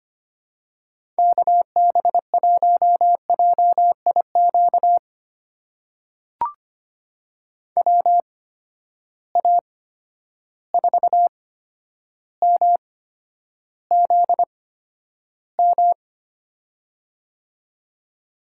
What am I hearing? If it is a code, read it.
Morse code, heard as a keyed tone, about 25 wpm, spaced as for 11 wpm, 700 Hz. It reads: KB1JIQ WA4MZM
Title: USA Call Signs - 6 Characters - 8x Character Spacing - 25wpm